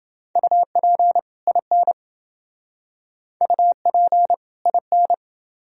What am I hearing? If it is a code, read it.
Morse code, heard as a keyed tone, about 28 wpm, 700 Hz. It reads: UP IN UP IN